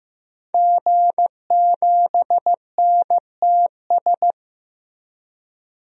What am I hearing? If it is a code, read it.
Morse code, heard as a keyed tone, about 15 wpm, 700 Hz. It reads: G7NTS
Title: DXCC-100 Call Signs - 5 Characters - 15wpm